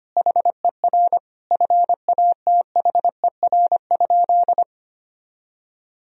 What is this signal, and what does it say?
Morse code, heard as a keyed tone, about 25 wpm, 700 Hz. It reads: HER FATHER?
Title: Sentences from Top 1000 Words - Review Part 4 - 25wpm